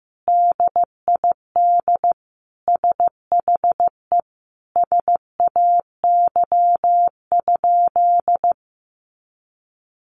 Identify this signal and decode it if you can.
Morse code, heard as a keyed tone, about 15 wpm, 700 Hz. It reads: DID SHE SAY?
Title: Sentences from Top 200 Words 15wpm